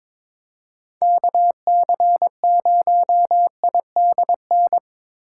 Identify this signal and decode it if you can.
Morse code, heard as a keyed tone, about 25 wpm, 700 Hz. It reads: KC0IDN
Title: Speed Racing - USA 2x3 Call Signs 22wpm